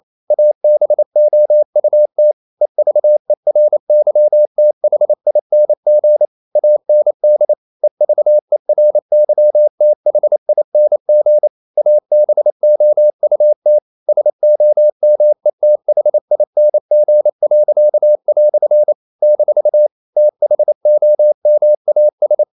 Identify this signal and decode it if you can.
Morse code, heard as a keyed tone, about 28 wpm, 600 Hz. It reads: ABOUT EVERYTHING AND EVERYTHING ABOUT SOMETHING." - THOMAS
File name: CW / Practice / EnglishQuotations0128